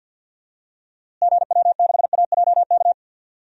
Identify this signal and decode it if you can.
Morse code, heard as a keyed tone, about 52 wpm, 700 Hz. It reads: GW6AJK
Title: DXCC-100 Call Signs - 6 Characters - Repeated 2x - 50wpm